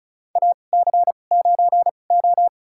Morse code, heard as a keyed tone, about 34 wpm, spaced as for 25 wpm, 700 Hz. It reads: AC9O
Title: USA Call Signs - 4 Characters - 2x Character Spacing - 35wpm